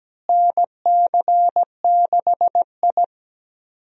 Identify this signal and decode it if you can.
Morse code, heard as a keyed tone, about 17 wpm, 700 Hz. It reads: NC6I
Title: USA Call Signs - 4 Characters - Repeated 2x - 17wpm